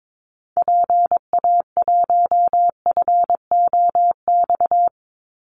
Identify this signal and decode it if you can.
Morse code, heard as a keyed tone, about 22 wpm, 700 Hz. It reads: PA1FOX
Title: Call Signs - 22wpm